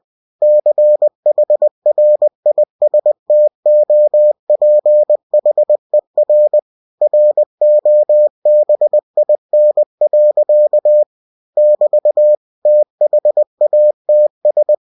Morse code, heard as a keyed tone, about 20 wpm, 600 Hz. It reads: CHRISTOPHER ROBIN. <BT> THATS
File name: cw-books / winnie-the-pooh / winnie-20wpm-page-0011